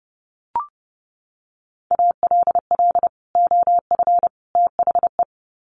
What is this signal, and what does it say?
Morse code, heard as a keyed tone, about 30 wpm, 700 Hz. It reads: ALL OF THE
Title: Sets of 3 Words 30wpm